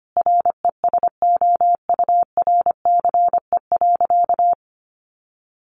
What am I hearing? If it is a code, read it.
Morse code, heard as a keyed tone, about 25 wpm, 700 Hz. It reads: RESOURCE.
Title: Sentences from Top 1300 Words 25wpm